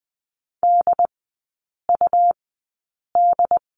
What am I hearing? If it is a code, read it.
Morse code, heard as a keyed tone, about 21 wpm, spaced as for 8 wpm, 690 Hz. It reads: DUD
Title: 145 Lesson 24e - Letter y - Three Characters 20wpm